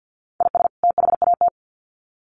Morse code, heard as a keyed tone, about 50 wpm, 700 Hz. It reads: IS THAT
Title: Sets of 2 Words 50wpm